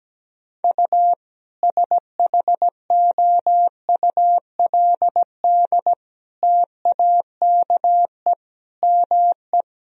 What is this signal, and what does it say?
Morse code, heard as a keyed tone, about 17 wpm, 700 Hz. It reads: U SHOULD TAKE ME
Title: Sentences from Top 300 Words and Abbreviations 17wpm